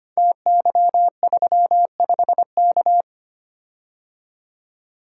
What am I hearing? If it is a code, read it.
Morse code, heard as a keyed tone, about 25 wpm, 700 Hz. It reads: TY35K
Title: Code Groups - 5 Letter:Numbers - 25wpm